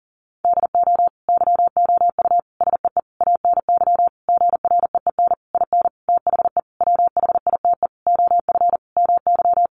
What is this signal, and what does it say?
Morse code, heard as a keyed tone, about 40 wpm, 700 Hz. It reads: DO YOU SEE ANY GREEN IN THE WHITE OF MY
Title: Sentences from Top 500 Words 40wpm